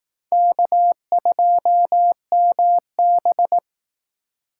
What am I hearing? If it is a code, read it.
Morse code, heard as a keyed tone, about 19 wpm, 700 Hz. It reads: K2MB